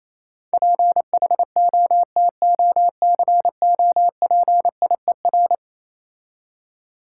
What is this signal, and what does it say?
Morse code, heard as a keyed tone, about 27 wpm, 690 Hz. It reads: PHOTOCOPIER
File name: Easy Words - Speed-Racing with Hints - 22wpm